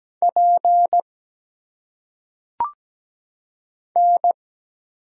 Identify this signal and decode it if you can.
Morse code, heard as a keyed tone, about 17 wpm, 700 Hz. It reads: P N
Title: Single Letters - Phonetic Pronunciation 17wpm